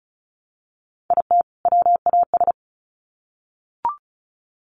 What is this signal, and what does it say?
Morse code, heard as a keyed tone, about 34 wpm, 700 Hz. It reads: IT WAS
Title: Sets of 2 Words 35wpm